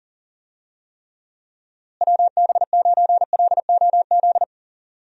Morse code, heard as a keyed tone, about 40 wpm, 700 Hz. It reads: WB9LOZ
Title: USA Call Signs - 6 Characters - Repeated 2x - 40wpm